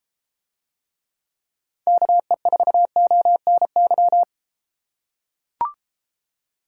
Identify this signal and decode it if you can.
Morse code, heard as a keyed tone, about 32 wpm, 700 Hz. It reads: KE4ONY